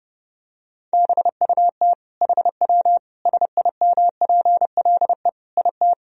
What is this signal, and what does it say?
Morse code, heard as a keyed tone, about 30 wpm, 700 Hz. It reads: BUT HW SIMPLE IT
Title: Sentences from Top 400 Words and Abbreviations 30wpm